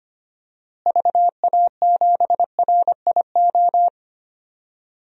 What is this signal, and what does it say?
Morse code, heard as a keyed tone, about 25 wpm, 700 Hz. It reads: VA7RIO
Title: DXCC-100 Call Signs - 6 Characters - Repeated 2x - 25wpm